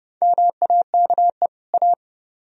Morse code, heard as a keyed tone, about 36 wpm, 700 Hz. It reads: MAKE A